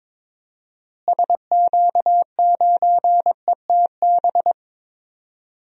Morse code, heard as a keyed tone, about 22 wpm, 700 Hz. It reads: SQ9ETB
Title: DXCC-100 Call Signs - 6 Characters - Repeated 2x - 22wpm